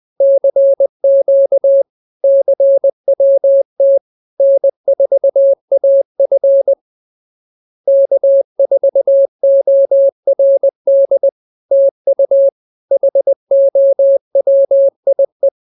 Morse code, heard as a keyed tone, about 20 wpm, 550 Hz. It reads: CQ CWT N4AF K4ORD TU HOWIE